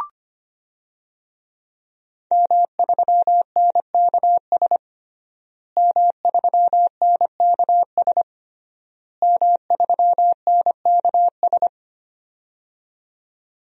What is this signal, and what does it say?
Morse code, heard as a keyed tone, about 25 wpm, 700 Hz. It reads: M3NKS M3NKS M3NKS
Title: DXCC-100 Call Signs - 5 Characters - Repeated 3x - 25wpm